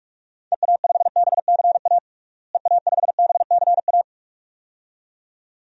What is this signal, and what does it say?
Morse code, heard as a keyed tone, about 47 wpm, 700 Hz. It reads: EA5BCA EA5BCA
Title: DXCC-100 Call Signs - 6 Characters - Repeated 2x - 45wpm